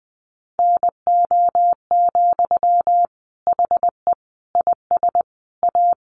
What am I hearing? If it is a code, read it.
Morse code, heard as a keyed tone, about 20 wpm, 700 Hz. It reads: NO, HE IS A
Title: Sentences from Top 400 Words 20wpm